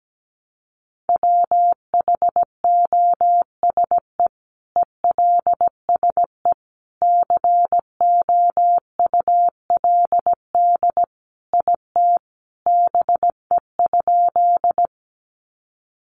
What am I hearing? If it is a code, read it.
Morse code, heard as a keyed tone, about 17 wpm, 700 Hz. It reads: WHOSE ELSE COULD IT BE?
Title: Sentences from Top 1200 Words 17wpm